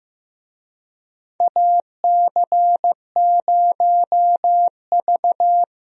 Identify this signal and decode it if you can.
Morse code, heard as a keyed tone, about 15 wpm, 700 Hz. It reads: AC0V
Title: USA Call Signs - 4 Characters - Repeated 3x - 15wpm